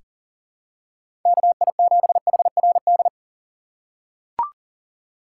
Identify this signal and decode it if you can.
Morse code, heard as a keyed tone, about 40 wpm, 700 Hz. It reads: KI7HRD
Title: USA Call Signs - 6 Characters - 2x Character Spacing - 40wpm